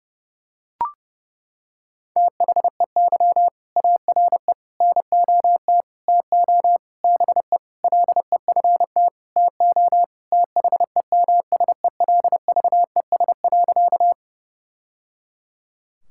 Morse code, heard as a keyed tone, about 30 wpm, 700 Hz. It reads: THEY ARE NOT TO BE LEFT TO THEMSELVES.
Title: Sentences from Top 1000 Words - Review Part 5 - 30wpm